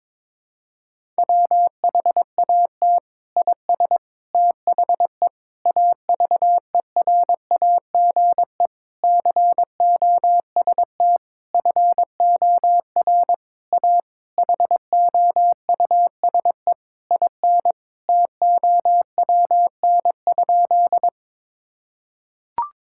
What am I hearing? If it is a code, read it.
Morse code, heard as a keyed tone, about 22 wpm, 700 Hz. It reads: WHAT IS THE AVERAGE COST FOR A HOUSE IN TOWN?
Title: Sentences from Top 1000 Words - Review Part 6 - 22wpm